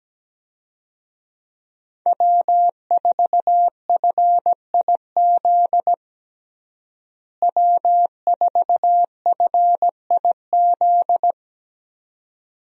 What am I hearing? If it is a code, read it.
Morse code, heard as a keyed tone, about 17 wpm, 700 Hz. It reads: W4FIZ W4FIZ
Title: USA Call Signs - 5 Characters - Repeated 2x - 17wpm